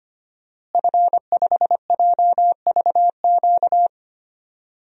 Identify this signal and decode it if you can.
Morse code, heard as a keyed tone, about 25 wpm, 700 Hz. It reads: F5JVQ
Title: DXCC-100 Call Signs - 5 Characters - Repeated 2x - 25wpm